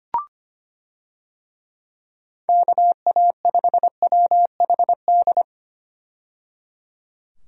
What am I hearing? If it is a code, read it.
Morse code, heard as a keyed tone, about 25 wpm, 700 Hz. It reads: KA5WHD
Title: USA Call Signs - 6 Characters - 25wpm